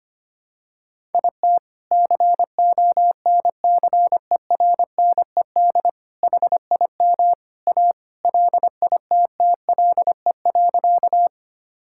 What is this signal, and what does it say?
Morse code, heard as a keyed tone, about 25 wpm, 700 Hz. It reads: IT CONCERNED HIM A LITTLE.
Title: Sentences from Top 1000 Words - Review Part 6 - 25wpm